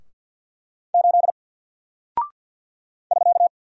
Morse code, heard as a keyed tone, about 48 wpm, 700 Hz. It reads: 8 <SK>
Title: Single Character (Letter:Number:Punctuation:Prosign) Alt 50wpm